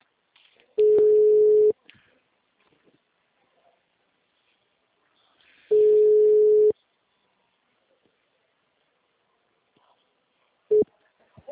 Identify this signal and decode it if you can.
Ringback tone (Europe, 425 Hz).